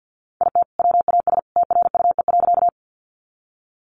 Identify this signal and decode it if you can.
Morse code, heard as a keyed tone, about 50 wpm, 700 Hz. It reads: IT WAS TRUE.
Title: Sentences from Top 400 Words 50wpm